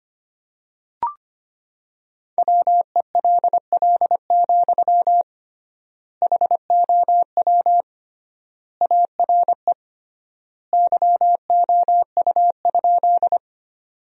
Morse code, heard as a keyed tone, about 25 wpm, 700 Hz. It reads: WELL, HOW ARE YOU?
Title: 3x Word-Spacing - Sentences from Top 200 Words - 25wpm